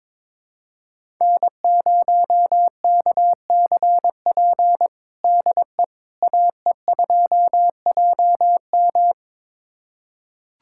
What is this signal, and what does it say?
Morse code, heard as a keyed tone, about 21 wpm, 700 Hz. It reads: N0KCP DE AE2JM
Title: USA Callsign DE Callsign 22wpm